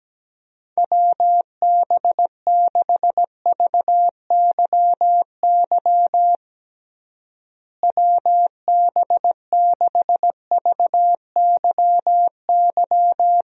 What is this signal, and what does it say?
Morse code, heard as a keyed tone, about 17 wpm, 700 Hz. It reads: WB6VYY WB6VYY
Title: USA Call Signs - 6 Characters - Repeated 2x - 17wpm